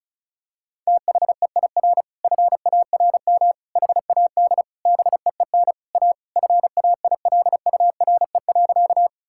Morse code, heard as a keyed tone, about 36 wpm, 700 Hz. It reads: THEIR FARM HAD BEEN A FAILURE.
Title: Sentences from Top 1000 Words - Review Part 6 - 35wpm